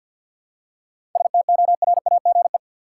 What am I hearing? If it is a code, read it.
Morse code, heard as a keyed tone, about 51 wpm, 700 Hz. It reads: STORAGE
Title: Seven Letter Words - 50wpm